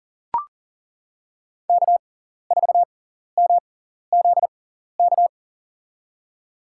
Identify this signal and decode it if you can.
Morse code, heard as a keyed tone, about 40 wpm, spaced as for 14 wpm, 700 Hz. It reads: K4MZK
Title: USA Call Signs - 5 Characters - 6x Character Spacing - 40wpm